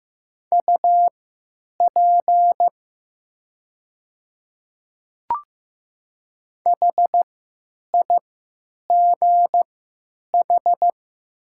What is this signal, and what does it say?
Morse code, heard as a keyed tone, about 15 wpm, spaced as for 9 wpm, 700 Hz. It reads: UP HIGH
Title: 3x Character Spacing - Top 200 Words - 15wpm